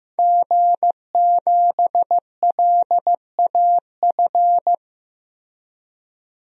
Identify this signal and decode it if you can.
Morse code, heard as a keyed tone, about 15 wpm, 700 Hz. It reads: G7LAF
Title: DXCC-100 Call Signs - 5 Characters - Repeated 2x - 15wpm